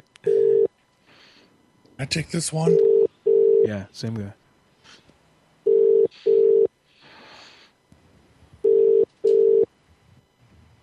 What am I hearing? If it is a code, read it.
Ringback tone (UK), 400 + 450 Hz, with speech.